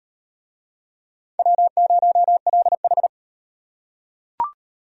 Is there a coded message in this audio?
Yes: Morse code from 1.4 to 3.1 s, heard as a keyed tone, about 38 wpm, 700 Hz. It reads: W0LH